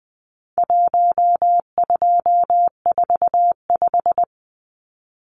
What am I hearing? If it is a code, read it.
Morse code, heard as a keyed tone, about 20 wpm, 700 Hz. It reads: 1245
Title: Numbers - Contest Style - 4-Digit 20wpm